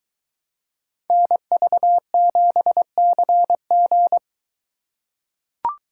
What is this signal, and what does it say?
Morse code, heard as a keyed tone, about 22 wpm, 700 Hz. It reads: NV7CG